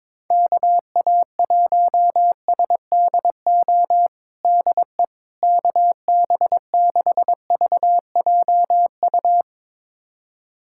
Morse code, heard as a keyed tone, about 21 wpm, 700 Hz. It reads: KA1SDO DE KB6VJU
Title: USA Callsign DE Callsign - 6 Characters - 22wpm